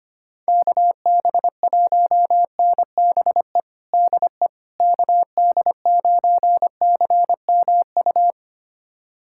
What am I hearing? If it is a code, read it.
Morse code, heard as a keyed tone, about 25 wpm, 700 Hz. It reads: KB1NBE DE KD9CMU
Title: USA Callsign DE Callsign - 6 Characters - 25wpm